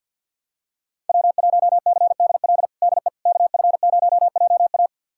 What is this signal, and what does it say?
Morse code, heard as a keyed tone, about 51 wpm, 700 Hz. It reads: W1CDL DE KF0JA
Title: USA Callsign DE Callsign - 5 Characters - 50wpm